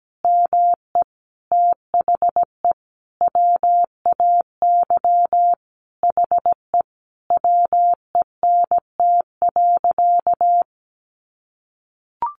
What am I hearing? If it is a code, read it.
Morse code, heard as a keyed tone, about 17 wpm, 700 Hz. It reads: ME THE WAY HE WENT.
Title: Sentences from Top 200 Words 17wpm